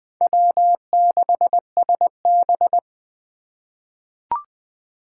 Morse code, heard as a keyed tone, about 20 wpm, 700 Hz. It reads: W6SB
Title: USA Call Signs - 4 Characters - 2x Character Spacing - 20wpm